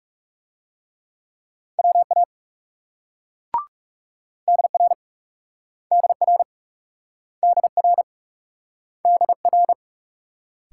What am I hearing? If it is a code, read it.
Morse code, heard as a keyed tone, about 38 wpm, 700 Hz. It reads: WA DR DR DR DR